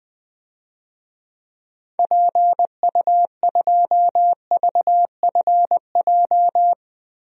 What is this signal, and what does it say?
Morse code, heard as a keyed tone, about 20 wpm, 700 Hz. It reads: PU2VFJ
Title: DXCC-100 Call Signs - 6 Characters - Repeated 2x - 20wpm